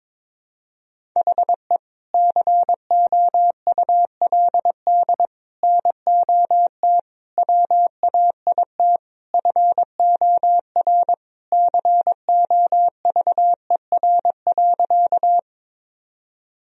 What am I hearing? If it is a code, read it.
Morse code, heard as a keyed tone, about 22 wpm, 700 Hz. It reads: HE COULD NOT WAIT FOR COVER.
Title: Sentences from Top 1000 Words - Review Part 1 - 22wpm